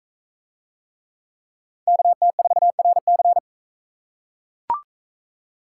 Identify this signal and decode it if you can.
Morse code, heard as a keyed tone, about 40 wpm, 700 Hz. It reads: KT4RC